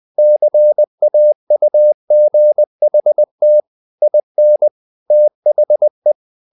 Morse code, heard as a keyed tone, about 20 wpm, 600 Hz. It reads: CAUGHT IN THE
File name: the old man and the sea part 2 20wpm